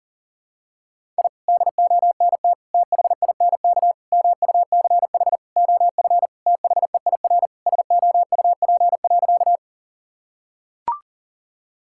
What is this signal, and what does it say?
Morse code, heard as a keyed tone, about 40 wpm, 700 Hz. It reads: I DONT THINK MUCH OF THEIR SOUP.
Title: Sentences from Top 1000 Words - Review Part 6 - 40wpm